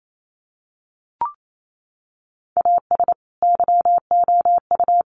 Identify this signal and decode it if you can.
Morse code, heard as a keyed tone, about 28 wpm, 700 Hz. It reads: AS YOU
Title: Sentences from Top 1100 Words 28wpm